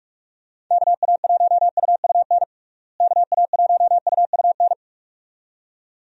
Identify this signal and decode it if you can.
Morse code, heard as a keyed tone, about 46 wpm, 700 Hz. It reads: KA1UUN KA1UUN
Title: USA Call Signs - 6 Characters - Repeated 2x - 45wpm